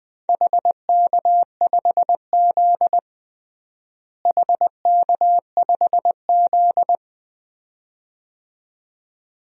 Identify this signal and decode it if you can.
Morse code, heard as a keyed tone, about 20 wpm, 700 Hz. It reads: HK5Z HK5Z
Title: DXCC-100 Call Signs - 4 Characters - Repeated 2x - 20wpm